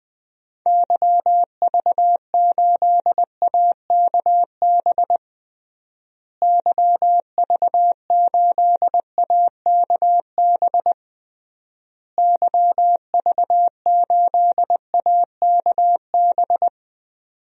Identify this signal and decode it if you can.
Morse code, heard as a keyed tone, about 20 wpm, 700 Hz. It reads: YV8AKB YV8AKB YV8AKB
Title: DXCC-100 Call Signs - 6 Characters - Repeated 3x - 20wpm